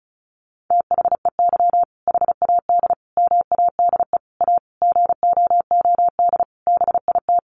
Morse code, heard as a keyed tone, about 34 wpm, 700 Hz. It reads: THEY HAD MADE A GOOD BIT
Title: Sentences from Top 900 Words 35wpm